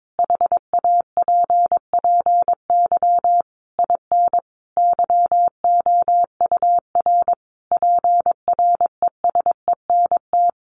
Morse code, heard as a keyed tone, about 22 wpm, 700 Hz. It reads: HAPPY IN YOUR PRESENT
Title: Sentences from Top 1200 Words 22wpm